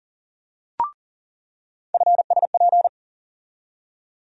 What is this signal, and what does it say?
Morse code, heard as a keyed tone, about 40 wpm, 700 Hz. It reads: FSP
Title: USA Call Sign Postfixes - 3 Letters 40wpm